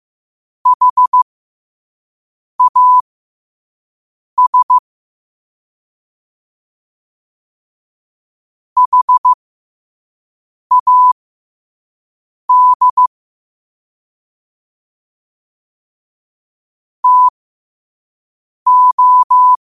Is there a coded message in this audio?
Yes: Morse code from 0.7 to 19.5 s, heard as a keyed tone, about 15 wpm, spaced as for 5 wpm, 1000 Hz. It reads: HAS HAD TO